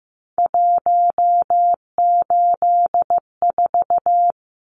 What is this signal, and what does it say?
Morse code, heard as a keyed tone, about 15 wpm, 700 Hz. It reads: 184
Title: Numbers - 3-Digit 15wpm